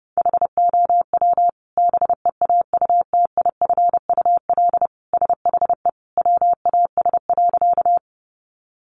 Morse code, heard as a keyed tone, about 30 wpm, 700 Hz. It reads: HOW BEAUTIFUL SHE WAS.
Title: Sentences from Top 900 Words 30wpm